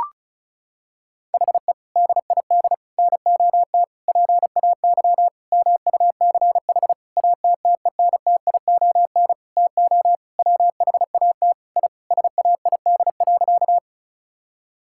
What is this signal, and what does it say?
Morse code, heard as a keyed tone, about 36 wpm, 700 Hz. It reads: HE DID NOT PAY MUCH ATTENTION TO WHAT I SAID.
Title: Sentences from Top 1000 Words - Review Part 6 - 35wpm